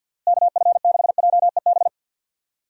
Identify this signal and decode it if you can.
Morse code, heard as a keyed tone, about 51 wpm, 700 Hz. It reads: KF6JEB
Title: USA Call Signs - 6 Characters - Repeated 2x - 50wpm